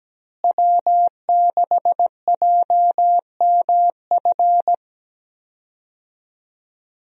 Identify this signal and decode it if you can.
Morse code, heard as a keyed tone, about 17 wpm, 700 Hz. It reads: W6JMF
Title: USA Call Signs - 5 Characters - Repeated 3x - 17wpm